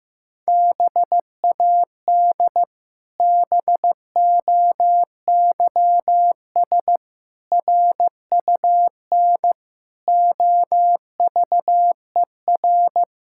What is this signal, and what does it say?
Morse code, heard as a keyed tone, about 16 wpm, 700 Hz. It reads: BAD BOYS RUN OVER
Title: Pangram Challenge v2 - 15wpm